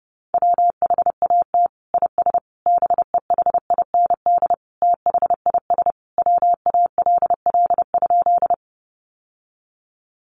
Morse code, heard as a keyed tone, about 28 wpm, 700 Hz. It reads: WHAT IS BEHIND THIS WALL?
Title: Sentences from Top 1100 Words 30wpm